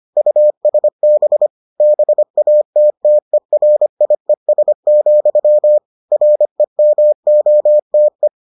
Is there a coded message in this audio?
Yes: Morse code from 0.2 to 8.3 s, heard as a keyed tone, about 25 wpm, 600 Hz. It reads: USB BATTERIES, REMOTE